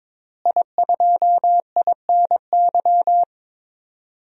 Morse code, heard as a keyed tone, about 22 wpm, 700 Hz. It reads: I2INY
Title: DXCC-100 Call Signs - 5 Characters - Repeated 3x - 22wpm